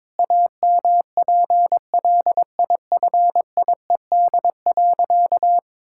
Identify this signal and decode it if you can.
Morse code, heard as a keyed tone, about 21 wpm, 700 Hz. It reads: AMPLIFIED.
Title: Pangram Challenge v2 - 22wpm